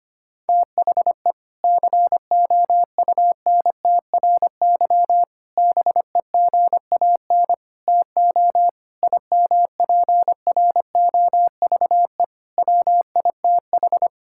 Morse code, heard as a keyed tone, about 25 wpm, 700 Hz. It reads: THE COUNTRY BEGAN TO IMPROVE WITH
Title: Sentences from Top 1000 Words - Review Part 5 - 25wpm